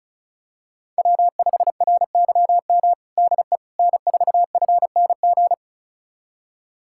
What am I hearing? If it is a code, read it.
Morse code, heard as a keyed tone, about 36 wpm, 700 Hz. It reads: W5RYM DE N4FNG
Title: USA Callsign DE Callsign - 5 Characters - 35wpm